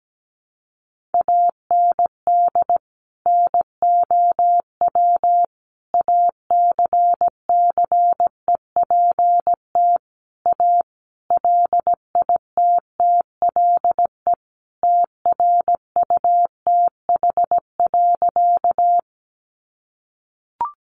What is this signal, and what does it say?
Morse code, heard as a keyed tone, about 17 wpm, 700 Hz. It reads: AND NOW ACCEPT A LITTLE TRUTH.
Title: Sentences from Top 1300 Words 17wpm